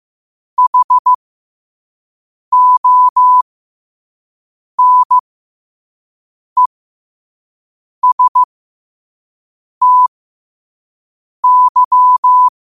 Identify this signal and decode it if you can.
Morse code, heard as a keyed tone, about 15 wpm, spaced as for 5 wpm, 1000 Hz. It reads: HONESTY
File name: qotdfiles / 5wpm